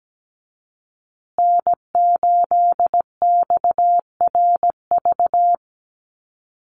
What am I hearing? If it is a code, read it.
Morse code, heard as a keyed tone, about 17 wpm, 700 Hz. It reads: N8XRV
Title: USA Call Signs Repeated 3x 17wpm